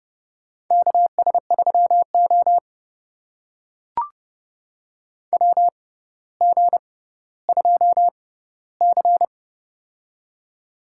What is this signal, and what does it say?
Morse code, heard as a keyed tone, about 30 wpm, spaced as for 15 wpm, 700 Hz. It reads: KS3O WG2C
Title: USA Call Signs - 4 Characters - 6x Character Spacing - 30wpm